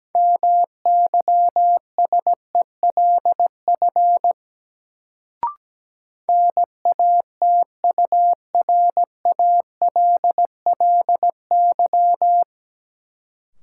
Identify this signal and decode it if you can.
Morse code, heard as a keyed tone, about 17 wpm, 700 Hz. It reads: MYSELF NATURALLY
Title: Sentences from Top 1300 Words - New Word Drill - 17wpm